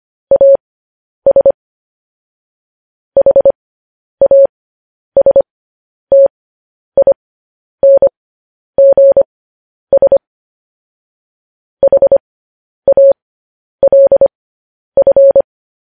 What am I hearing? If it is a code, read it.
Morse code, heard as a keyed tone, about 25 wpm, spaced as for 10 wpm, 550 Hz. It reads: AS HASTINGS HALF
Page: News Headlines in Morse Code at 10 WPM